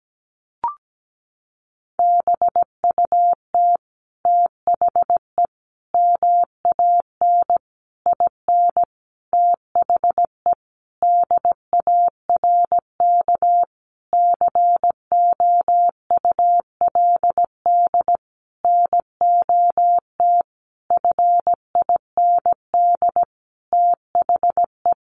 Morse code, heard as a keyed tone, about 17 wpm, 700 Hz. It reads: BUT THE MAN IN THE DARK COULD NOT FIND THE
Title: Sentences from Top 600 Words 17wpm